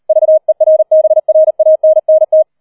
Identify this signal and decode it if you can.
Morse code, heard as a keyed tone, about 38 wpm, 630 Hz. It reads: VERBRANNT
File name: 004 Vierter Streich 40wpm